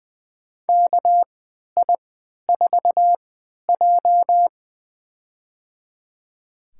Morse code, heard as a keyed tone, about 20 wpm, spaced as for 11 wpm, 700 Hz. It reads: KI4J
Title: USA Call Signs - 4 Characters - 3x Character Spacing - 20wpm